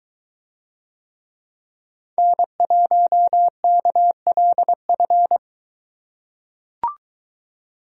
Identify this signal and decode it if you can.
Morse code, heard as a keyed tone, about 21 wpm, 700 Hz. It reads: N1KLF